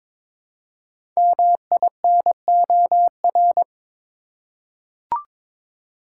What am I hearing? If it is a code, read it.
Morse code, heard as a keyed tone, about 23 wpm, 700 Hz. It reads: MINOR
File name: 5 Letter Words - First 3 Letters Given - 22wpm